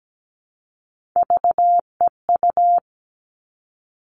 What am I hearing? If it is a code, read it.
Morse code, heard as a keyed tone, about 17 wpm, 700 Hz. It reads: VEU
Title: All Cut Numbers - 3-Digit 17wpm